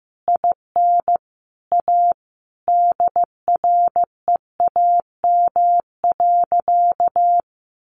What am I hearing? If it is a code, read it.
Morse code, heard as a keyed tone, about 15 wpm, 700 Hz. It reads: IN A DREAM.
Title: Sentences from Top 1500 Words 15wpm